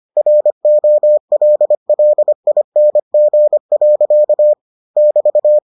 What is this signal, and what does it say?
Morse code, heard as a keyed tone, about 25 wpm, 600 Hz. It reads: ROLLING. <BT>